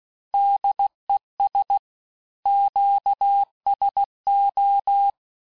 Morse code, heard as a keyed tone, about 16 wpm, 800 Hz. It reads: DES QSO
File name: cours-du-lundi files / cours-1592--teste-FIN-de-QSO-vitesse-15mpm